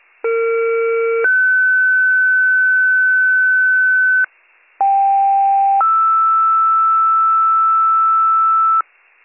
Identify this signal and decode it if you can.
Two-tone page, heard as a one-second tone, then a three-second tone, about 460 then 1500 Hz, about 790 then 1320 Hz, with noise. It is heard from 0.2 to 8.8 s.